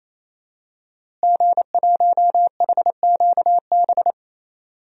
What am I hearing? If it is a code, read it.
Morse code, heard as a keyed tone, about 27 wpm, 700 Hz. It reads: G1HQB
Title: DXCC-100 Call Signs - 5 Characters - Repeated 3x - 28wpm